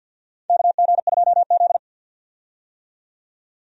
Morse code, heard as a keyed tone, about 52 wpm, 700 Hz. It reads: KG2Z
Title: USA Call Signs - 4 Characters - 50wpm